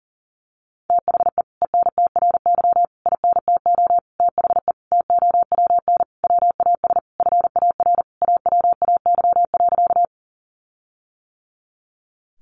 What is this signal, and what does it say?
Morse code, heard as a keyed tone, about 40 wpm, 700 Hz. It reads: THE ENTRY INTO THE TOWN WAS FAR AWAY.
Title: Sentences from Top 1500 Words 40wpm